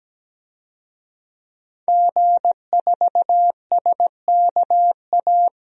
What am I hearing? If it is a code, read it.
Morse code, heard as a keyed tone, about 17 wpm, 700 Hz. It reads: G4SKA
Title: DXCC-100 Call Signs - 5 Characters - 17wpm